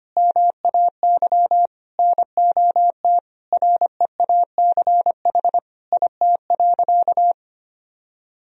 Morse code, heard as a keyed tone, about 25 wpm, 700 Hz. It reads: MAY NOT REACH IT.
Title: Sentences from Top 1000 Words - Review Part 2 - 25wpm